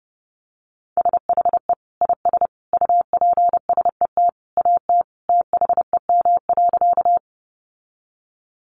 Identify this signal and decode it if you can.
Morse code, heard as a keyed tone, about 30 wpm, 700 Hz. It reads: SHE IS UPSET AT THEM.
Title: Sentences from Top 800 Words 30wpm